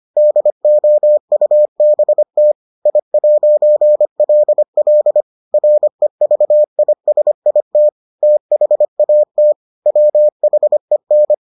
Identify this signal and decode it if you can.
Morse code, heard as a keyed tone, about 25 wpm, 600 Hz. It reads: DOUBT I'LL REVISIT THAT WHEN